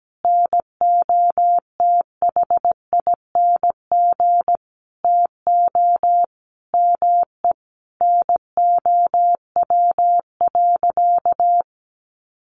Morse code, heard as a keyed tone, about 17 wpm, 700 Hz. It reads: NOTHING TO ME NOW.
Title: Sentences from Top 500 Words 17wpm